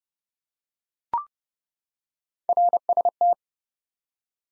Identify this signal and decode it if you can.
Morse code, heard as a keyed tone, about 30 wpm, 700 Hz. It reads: RST
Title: Top 90 Abbr 30WPM